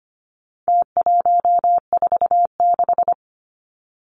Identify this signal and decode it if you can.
Morse code, heard as a keyed tone, about 25 wpm, 700 Hz. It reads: T146
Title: Numbers - Contest Style - 4-Digit 25wpm